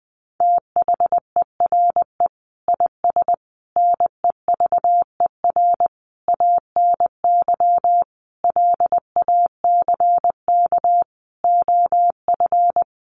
Morse code, heard as a keyed tone, about 19 wpm, 700 Hz. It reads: THERE IS NEVER ANY LACK OF
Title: Sentences from Top 1100 Words 20wpm